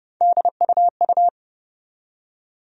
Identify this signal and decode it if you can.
Morse code, heard as a keyed tone, about 32 wpm, 700 Hz. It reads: DUU
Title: USA Call Sign Postfixes - 3 Letters 30wpm